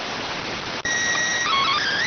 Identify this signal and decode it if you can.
SSTV (slow-scan television), PD 290.